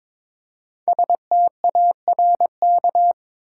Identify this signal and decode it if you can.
Morse code, heard as a keyed tone, about 22 wpm, 700 Hz. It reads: STARK
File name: 5 Letter Words - First 3 Letters Given - 22wpm